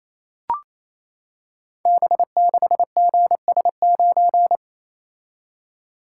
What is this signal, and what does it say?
Morse code, heard as a keyed tone, about 27 wpm, 700 Hz. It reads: B6GS9
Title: Code Groups - 5 Letter:Numbers - 28wpm